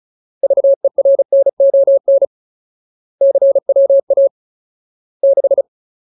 Morse code, heard as a keyed tone, about 34 wpm, 550 Hz. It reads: VERNON CWA 6
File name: Ultimate Word List / 35WPM Ultimate